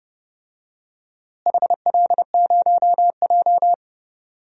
Morse code, heard as a keyed tone, about 28 wpm, 700 Hz. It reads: HL0J
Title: DXCC-100 Call Signs - 4 Characters - Repeated 3x - 30wpm